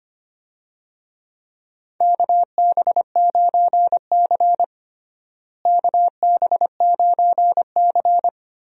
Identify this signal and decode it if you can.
Morse code, heard as a keyed tone, about 25 wpm, 700 Hz. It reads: KB9C KB9C
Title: USA Call Signs - 4 Characters - Repeated 3x - 25wpm